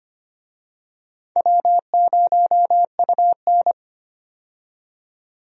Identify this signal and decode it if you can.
Morse code, heard as a keyed tone, about 25 wpm, 700 Hz. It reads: W0UN
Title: USA Call Signs - 4 Characters - 25wpm